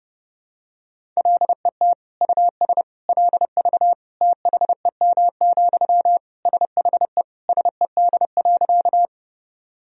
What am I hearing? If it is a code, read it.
Morse code, heard as a keyed tone, about 28 wpm, 700 Hz. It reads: LET US LV THEM, SHE SED.
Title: Sentences from Top 400 Words and Abbreviations 30wpm